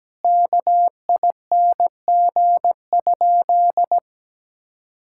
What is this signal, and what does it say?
Morse code, heard as a keyed tone, about 17 wpm, 700 Hz. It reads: KING?
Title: Sentences from Top 1000 Words - Review Part 5 - 17wpm